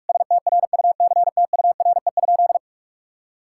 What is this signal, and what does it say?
Morse code, heard as a keyed tone, about 46 wpm, 700 Hz. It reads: STRUCTURE?
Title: Sentences from Top 1000 Words - Review Part 6 - 45wpm